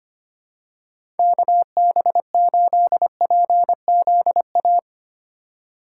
Morse code, heard as a keyed tone, about 25 wpm, 700 Hz. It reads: KB8PZA
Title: USA Call Signs - 2x Character Spacing - 25wpm